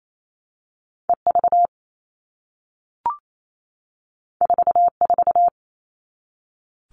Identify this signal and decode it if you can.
Morse code, heard as a keyed tone, about 28 wpm, 700 Hz. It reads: EV 44
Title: All Cut Numbers - 2-Digit 28wpm